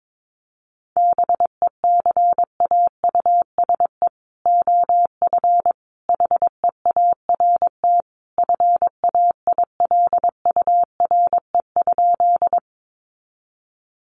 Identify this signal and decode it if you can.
Morse code, heard as a keyed tone, about 22 wpm, 700 Hz. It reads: BECAUSE OF HEART FAILURE?
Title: Sentences from Top 1200 Words 22wpm